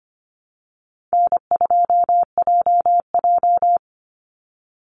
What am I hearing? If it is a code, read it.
Morse code, heard as a keyed tone, about 25 wpm, 700 Hz. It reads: N2JJ
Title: Easy Call Signs Repeated 3x - 25wpm